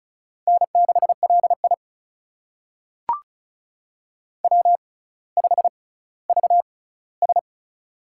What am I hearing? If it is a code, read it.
Morse code, heard as a keyed tone, about 36 wpm, spaced as for 18 wpm, 700 Hz. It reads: N6LI W5VS
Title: USA Call Signs - 4 Characters - 6x Character Spacing - 35wpm